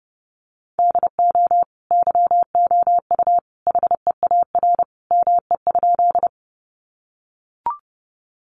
Morse code, heard as a keyed tone, about 30 wpm, 700 Hz. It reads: DO YOU HEAR ME?
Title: Sentences from Top 300 Words 30wpm